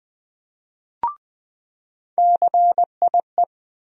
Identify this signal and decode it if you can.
Morse code, heard as a keyed tone, about 20 wpm, 700 Hz. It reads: CIE
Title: USA Call Sign Postfixes - 3 Letters 20wpm